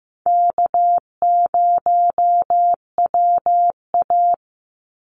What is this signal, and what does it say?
Morse code, heard as a keyed tone, about 15 wpm, 700 Hz. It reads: K0WA
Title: Easy Call Signs Repeated 3x - 15wpm